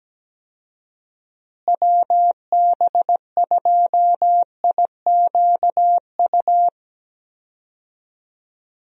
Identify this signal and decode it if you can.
Morse code, heard as a keyed tone, about 17 wpm, 700 Hz. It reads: WB2IQU
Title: USA Call Signs - 6 Characters - Repeated 3x - 17wpm